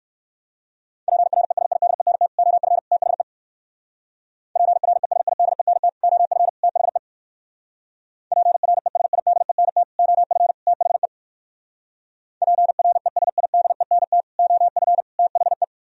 Morse code, heard as a keyed tone, about 58 wpm, 700 Hz. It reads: PRESIDENT OF THE PRESIDENT OF THE PRESIDENT OF THE PRESIDENT OF THE